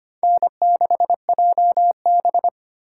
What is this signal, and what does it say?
Morse code, heard as a keyed tone, about 29 wpm, 700 Hz. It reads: N6JB
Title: Speed Racing - USA 1x2 Call Signs 25wpm